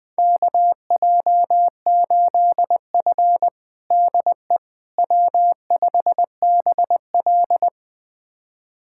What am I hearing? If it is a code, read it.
Morse code, heard as a keyed tone, about 20 wpm, 700 Hz. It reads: KJ8F DE W5BL
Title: USA Callsign DE Callsign - 4 Characters - 20wpm